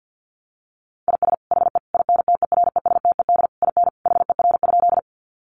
Morse code, heard as a keyed tone, about 50 wpm, 700 Hz. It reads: IS HE INTERESTED IN HER?
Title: Sentences from Top 900 Words 50wpm